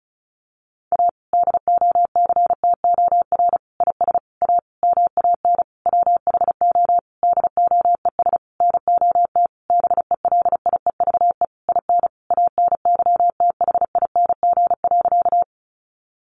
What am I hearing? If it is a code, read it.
Morse code, heard as a keyed tone, about 34 wpm, 700 Hz. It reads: A DOCTOR IS A MAN WHO DOES NOT BELIEVE IN ANYTHING.
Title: Sentences from Top 1500 Words 35wpm